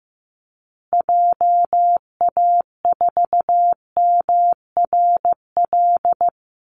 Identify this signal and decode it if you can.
Morse code, heard as a keyed tone, about 15 wpm, 700 Hz. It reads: JA4MRL
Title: Call Signs Repeated 3x - 15wpm